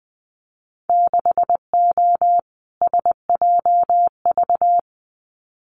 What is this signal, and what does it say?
Morse code, heard as a keyed tone, about 20 wpm, 700 Hz. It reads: 6O SJV
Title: Winter Field Day Exchanges Encore 20wpm